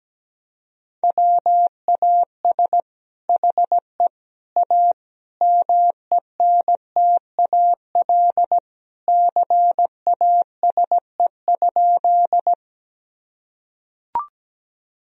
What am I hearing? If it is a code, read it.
Morse code, heard as a keyed tone, about 17 wpm, 700 Hz. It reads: WAS HE A MENTAL CASE?
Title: Sentences from Top 1000 Words - Review Part 4 - 17wpm